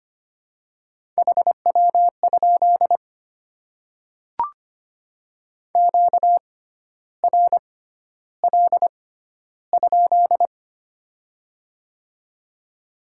Morse code, heard as a keyed tone, about 25 wpm, spaced as for 12 wpm, 700 Hz. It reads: HW? QRL?
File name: Top 90 QSO Elements - 6x Character Spacing - 25wpm